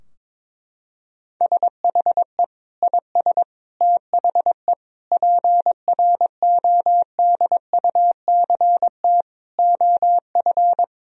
Morse code, heard as a keyed tone, about 22 wpm, 700 Hz. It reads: SHE IS THE PRODUCT OF
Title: Sentences from Top 1000 Words - Review Part 2 - 22wpm